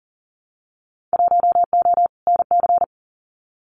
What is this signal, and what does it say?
Morse code, heard as a keyed tone, about 38 wpm, 700 Hz. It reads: 1O NC
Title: Winter Field Day Exchanges 40wpm